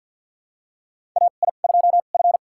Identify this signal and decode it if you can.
Morse code, heard as a keyed tone, about 51 wpm, spaced as for 36 wpm, 700 Hz. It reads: AI2F
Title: USA Call Signs - 4 Characters - 2x Character Spacing - 50wpm